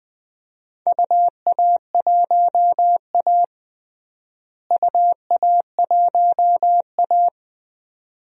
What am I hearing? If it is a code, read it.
Morse code, heard as a keyed tone, about 20 wpm, 700 Hz. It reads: UA1A UA1A
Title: DXCC-100 Call Signs - 4 Characters - Repeated 2x - 20wpm